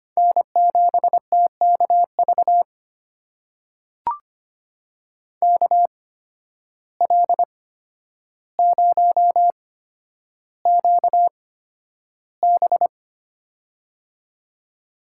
Morse code, heard as a keyed tone, about 25 wpm, spaced as for 11 wpm, 700 Hz. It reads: N7TKV KL0QB
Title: USA Call Signs - 5 Characters - 8x Character Spacing - 25wpm